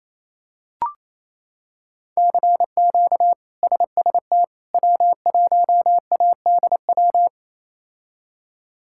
Morse code, heard as a keyed tone, about 27 wpm, 700 Hz. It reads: CQ SST W1ADW
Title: K1USN SST QSO Practice - 28wpm